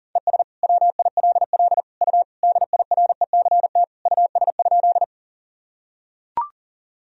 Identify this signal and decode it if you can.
Morse code, heard as a keyed tone, about 40 wpm, 700 Hz. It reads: ES WILL U DIRECT US?
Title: Sentences from Top 400 Words and Abbreviations 40wpm